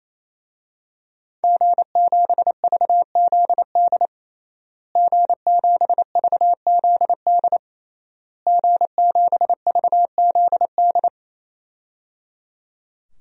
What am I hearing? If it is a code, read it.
Morse code, heard as a keyed tone, about 27 wpm, 700 Hz. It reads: G7VZD G7VZD G7VZD
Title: DXCC-100 Call Signs - 5 Characters - Repeated 3x - 28wpm